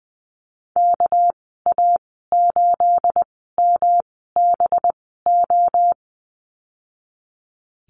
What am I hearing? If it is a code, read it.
Morse code, heard as a keyed tone, about 20 wpm, spaced as for 14 wpm, 700 Hz. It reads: KA8MBO
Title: 238 Lesson 38g - Letter q - Call Signs 20wpm